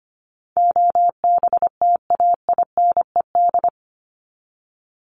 Morse code, heard as a keyed tone, about 25 wpm, 700 Hz. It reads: OBTAINED
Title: ED Suffix - Top 100 Words with Base Words 25wpm